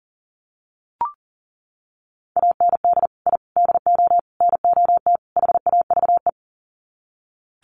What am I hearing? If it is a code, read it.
Morse code, heard as a keyed tone, about 40 wpm, 700 Hz. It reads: AND I DO NOT HAVE
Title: Sets of 5 Words Encore 40wpm